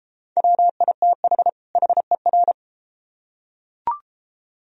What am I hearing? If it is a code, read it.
Morse code, heard as a keyed tone, about 32 wpm, 700 Hz. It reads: WITH HER